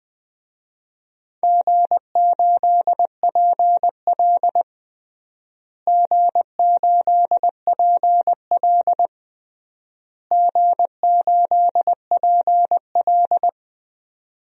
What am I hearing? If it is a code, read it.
Morse code, heard as a keyed tone, about 19 wpm, 700 Hz. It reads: G8PL G8PL G8PL
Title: DXCC-100 Call Signs - 4 Characters - Repeated 3x - 20wpm